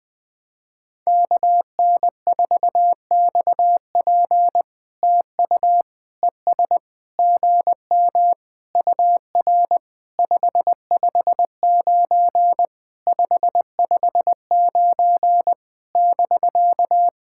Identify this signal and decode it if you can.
Morse code, heard as a keyed tone, about 20 wpm, 700 Hz. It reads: KN4XP TU ES GM UR 559 559 <BK>
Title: POTA QSO Practice - 20wpm